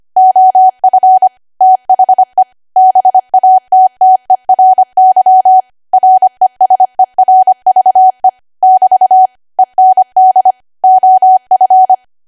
Morse code, heard as a keyed tone, about 25 wpm, 750 Hz. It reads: OF THE BATTERY RESERVE <BT> END OF
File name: Archive / 25 WPM / 25WPM